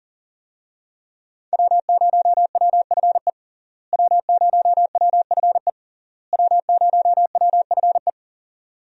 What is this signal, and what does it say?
Morse code, heard as a keyed tone, about 40 wpm, 700 Hz. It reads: W0WFE W0WFE W0WFE
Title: USA Call Signs - 5 Characters - Repeated 3x - 40wpm